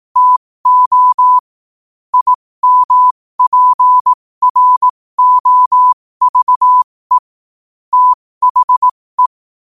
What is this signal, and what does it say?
Morse code, heard as a keyed tone, about 18 wpm, spaced as for 15 wpm, 1000 Hz. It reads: TO IMPROVE THE
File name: qotdfiles / 15wpm